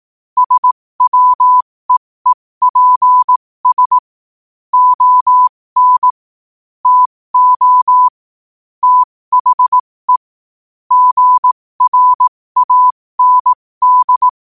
Morse code, heard as a keyed tone, about 18 wpm, spaced as for 15 wpm, 1000 Hz. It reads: SWEEPS ON TO THE GRAND
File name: qotdfiles / 15wpm